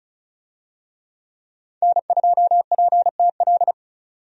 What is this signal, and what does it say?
Morse code, heard as a keyed tone, about 36 wpm, 700 Hz. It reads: N2PTL
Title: USA Call Signs - 5 Characters - 35wpm